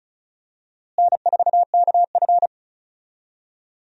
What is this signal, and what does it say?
Morse code, heard as a keyed tone, about 36 wpm, 700 Hz. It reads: N4KF
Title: USA Call Signs - 4 Characters - Repeated 2x - 35wpm